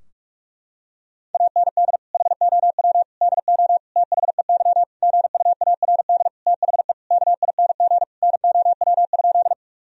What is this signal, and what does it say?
Morse code, heard as a keyed tone, about 42 wpm, 700 Hz. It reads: AND HOW DO THEY GUARD THE KING NOW?
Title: Sentences from Top 1000 Words - Review Part 4 - 45wpm